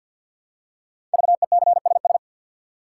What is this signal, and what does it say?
Morse code, heard as a keyed tone, about 48 wpm, 700 Hz. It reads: VECSS
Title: Code Groups - 5 Letters - 50wpm